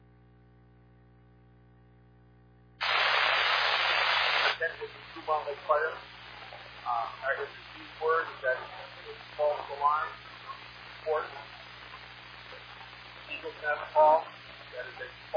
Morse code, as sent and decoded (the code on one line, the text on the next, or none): none